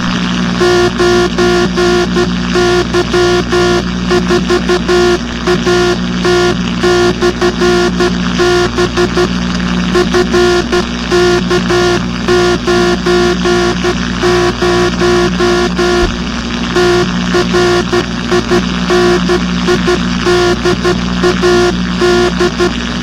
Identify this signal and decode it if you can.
Morse code, heard as a keyed tone, about 12 wpm, 360 Hz, with noise. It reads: 9Y4AT/B FK90 TRINIDAD